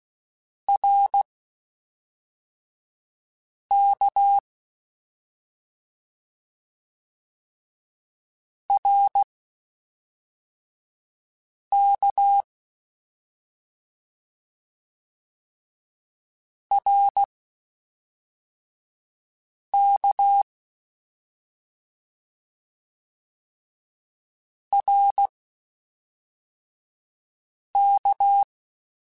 Morse code, heard as a keyed tone, about 16 wpm, spaced as for 3 wpm, 800 Hz. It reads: RK RK RK RK
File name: Short Runs / Short Runs-mp3s / Balanced @5